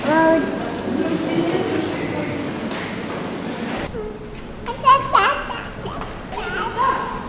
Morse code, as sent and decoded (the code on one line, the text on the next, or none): none